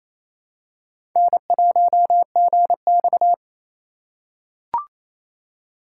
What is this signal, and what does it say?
Morse code, heard as a keyed tone, about 27 wpm, 700 Hz. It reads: N1GX
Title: USA Call Signs - 4 Characters - 3x Character Spacing - 28wpm